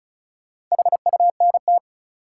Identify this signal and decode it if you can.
Morse code, heard as a keyed tone, about 36 wpm, 700 Hz. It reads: HUNT
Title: Sentences from Top 1600 Words - New Word Drill Alt - 35wpm